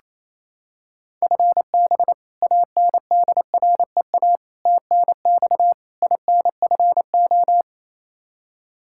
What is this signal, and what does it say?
Morse code, heard as a keyed tone, about 27 wpm, 700 Hz. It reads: FB ANDREA TNX INFO
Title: Practice QSO - Exchange 3 - More Info - 28wpm